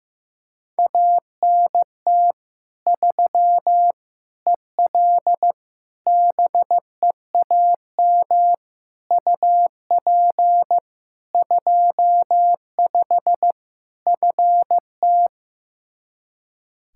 Morse code, heard as a keyed tone, about 15 wpm, 700 Hz. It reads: ANT 3 EL BEAM UP 25 FT
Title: K1USN SST QSO Practice 15wpm